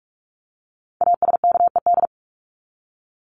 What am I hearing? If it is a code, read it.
Morse code, heard as a keyed tone, about 44 wpm, 700 Hz. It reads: ASKED
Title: ED Suffix - Top 100 Words with Base Words 45wpm